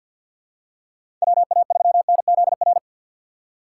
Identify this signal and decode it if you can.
Morse code, heard as a keyed tone, about 46 wpm, 700 Hz. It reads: WA3NZR
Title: K1USN SST QSO Practice - 50wpm